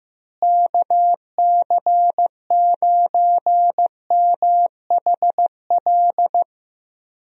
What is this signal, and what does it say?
Morse code, heard as a keyed tone, about 15 wpm, 700 Hz. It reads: KC9MHL
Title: USA Call Signs - 6 Characters - Repeated 2x - 15wpm